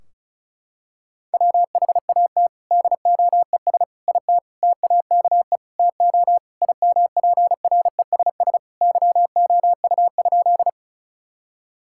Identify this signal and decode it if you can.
Morse code, heard as a keyed tone, about 36 wpm, 700 Hz. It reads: WHAT DOES IT TAKE TO IMPRESS YOU?
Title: Sentences from Top 1600 Words 35wpm